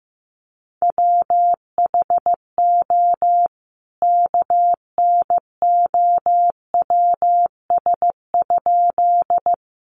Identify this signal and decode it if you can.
Morse code, heard as a keyed tone, about 15 wpm, 700 Hz. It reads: WHO KNOWS?
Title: A Sound Of Thunder 15wpm